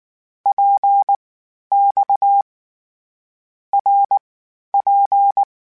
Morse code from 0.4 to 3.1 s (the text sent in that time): PX